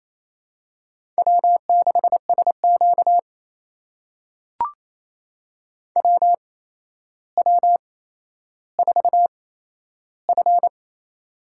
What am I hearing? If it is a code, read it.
Morse code, heard as a keyed tone, about 28 wpm, spaced as for 12 wpm, 700 Hz. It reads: W6SQ WW4F
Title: USA Call Signs - 4 Characters - 8x Character Spacing - 28wpm